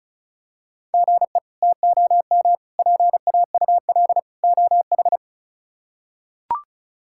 Morse code, heard as a keyed tone, about 36 wpm, 700 Hz. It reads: GE TOM PAUL OH
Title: K1USN SST QSO Practice - 35wpm